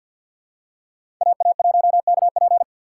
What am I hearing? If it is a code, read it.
Morse code, heard as a keyed tone, about 52 wpm, 700 Hz. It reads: AA1KP
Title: USA Call Signs - 5 Characters - 3x Character Spacing - 50wpm